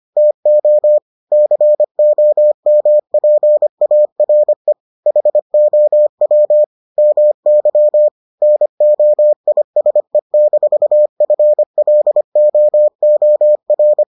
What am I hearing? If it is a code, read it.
Morse code, heard as a keyed tone, about 25 wpm, 600 Hz. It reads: TO COMPARE HOW MY NOISE-FLOOR